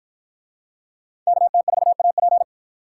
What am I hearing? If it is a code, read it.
Morse code, heard as a keyed tone, about 49 wpm, 700 Hz. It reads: KT4AP